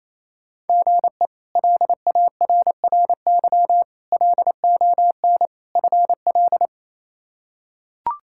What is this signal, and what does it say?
Morse code, heard as a keyed tone, about 28 wpm, 700 Hz. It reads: GE LARRY LON FL
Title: K1USN SST QSO Practice - 28wpm